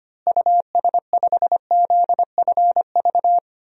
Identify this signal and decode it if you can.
Morse code, heard as a keyed tone, about 25 wpm, 700 Hz. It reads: US5ZFV
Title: DXCC-100 Call Signs - 6 Characters - Repeated 3x - 25wpm